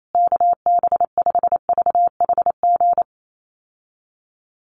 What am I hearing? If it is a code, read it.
Morse code, heard as a keyed tone, about 28 wpm, 700 Hz. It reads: KB5VHG
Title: USA Call Signs Repeated 3x 28wpm